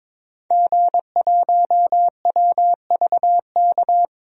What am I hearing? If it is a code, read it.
Morse code, heard as a keyed tone, about 22 wpm, 700 Hz. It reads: G1WVK
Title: DXCC-100 Call Signs - 5 Characters - 22wpm